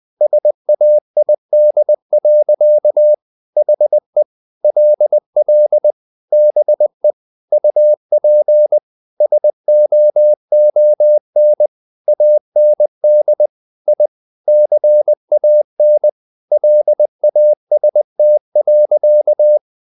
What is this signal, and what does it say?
Morse code, heard as a keyed tone, about 20 wpm, 600 Hz. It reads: SAID. HE LL BE UP SOON AND I CAN LAST.
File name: the old man and the sea part 2 20wpm